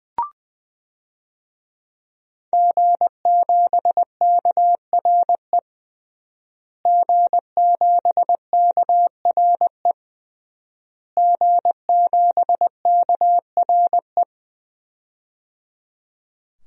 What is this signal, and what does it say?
Morse code, heard as a keyed tone, about 21 wpm, 700 Hz. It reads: G7KRE G7KRE G7KRE
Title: DXCC-100 Call Signs - 5 Characters - Repeated 3x - 20wpm